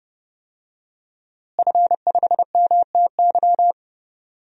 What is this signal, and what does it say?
Morse code, heard as a keyed tone, about 30 wpm, 700 Hz. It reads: F5MTY
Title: DXCC-100 Call Signs - 5 Characters - Repeated 3x - 30wpm